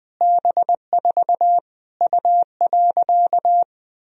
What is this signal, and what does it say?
Morse code, heard as a keyed tone, about 20 wpm, 700 Hz. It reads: B4 U.
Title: Sentences from Top 200 Words and Abbreviations 20wpm